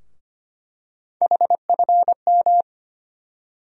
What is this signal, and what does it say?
Morse code, heard as a keyed tone, about 25 wpm, 700 Hz. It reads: HFM
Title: USA Call Sign Postfixes - 3 Letters 25wpm